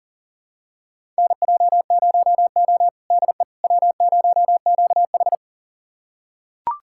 Morse code, heard as a keyed tone, about 40 wpm, 700 Hz. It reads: NJ0O DE W0QH
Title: USA Callsign DE Callsign - 4 Characters - 40wpm